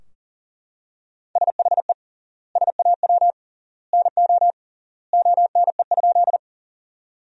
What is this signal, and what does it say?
Morse code, heard as a keyed tone, about 40 wpm, 700 Hz. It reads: SHE SAW NO ONE?
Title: 3x Word-Spacing - Sentences from Top 300 Words - 40wpm